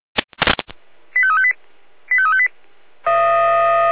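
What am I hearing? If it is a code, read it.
Five-tone selective call (ZVEI), heard as quick runs of five tones, tones of about 70 ms. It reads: 85358 85358
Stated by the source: ZVEI five-tone selective call received by SDR.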